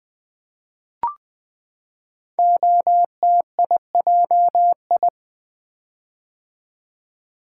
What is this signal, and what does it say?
Morse code, heard as a keyed tone, about 20 wpm, 700 Hz. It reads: OTIJI
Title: Code Groups - 5 Letters - 20wpm